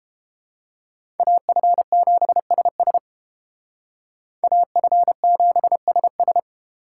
Morse code, heard as a keyed tone, about 28 wpm, 700 Hz. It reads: AF7SS AF7SS